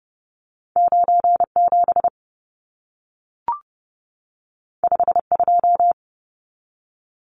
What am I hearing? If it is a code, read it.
Morse code, heard as a keyed tone, about 30 wpm, 700 Hz. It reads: 97 52
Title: Numbers - 2-Digit 30wpm